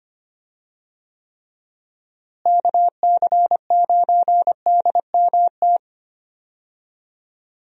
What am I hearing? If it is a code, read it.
Morse code, heard as a keyed tone, about 25 wpm, 700 Hz. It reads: KC9DMT
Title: USA Call Signs - 6 Characters - Repeated 2x - 25wpm